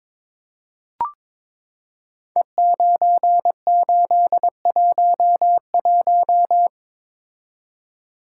Morse code, heard as a keyed tone, about 22 wpm, 700 Hz. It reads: E9811
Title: Code Groups - 5 Letter:Numbers - 22wpm